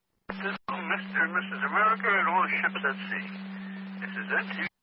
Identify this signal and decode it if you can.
Emergency Alert System (SAME) tones, heard as harsh buzzing bursts, with speech. It is heard from 0.9 to 4.5 s.